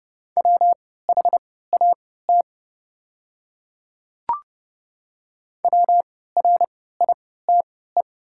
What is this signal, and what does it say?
Morse code, heard as a keyed tone, about 31 wpm, spaced as for 17 wpm, 700 Hz. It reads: WHAT WRITE
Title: 3x Character Spacing - Top 100 Words - 30wpm